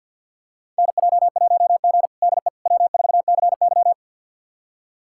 Morse code, heard as a keyed tone, about 47 wpm, 700 Hz. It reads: NJ1G DE W4CY
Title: USA Callsign DE Callsign - 4 Characters - 50wpm